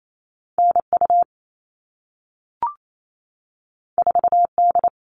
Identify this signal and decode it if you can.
Morse code, heard as a keyed tone, about 28 wpm, 700 Hz. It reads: NU 4D
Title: All Cut Numbers - 2-Digit 28wpm